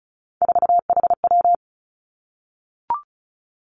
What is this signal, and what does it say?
Morse code, heard as a keyed tone, about 34 wpm, 700 Hz. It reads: 4HW
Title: Three Challenging Letter-Number Combinations 35wpm